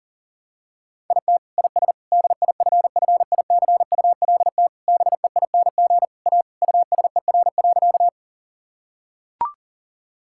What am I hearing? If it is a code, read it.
Morse code, heard as a keyed tone, about 40 wpm, 700 Hz. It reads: IT IS DIFFICULT BEING A USER.
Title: Sentences from Top 1000 Words - Review Part 5 - 40wpm